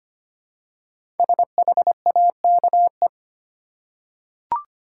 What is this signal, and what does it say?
Morse code, heard as a keyed tone, about 25 wpm, 700 Hz. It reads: SHAKE
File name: 5 Letter Words - First 3 Letters Given - 25wpm